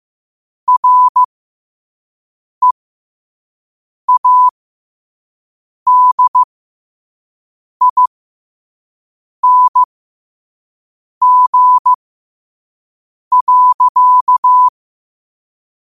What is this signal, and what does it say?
Morse code, heard as a keyed tone, about 15 wpm, spaced as for 5 wpm, 1000 Hz. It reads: READING.
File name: qotdfiles / 5wpm